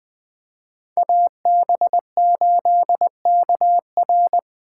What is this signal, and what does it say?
Morse code, heard as a keyed tone, about 20 wpm, 700 Hz. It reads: AB8KR
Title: Speed Racing - USA 2x2 Call Signs 17wpm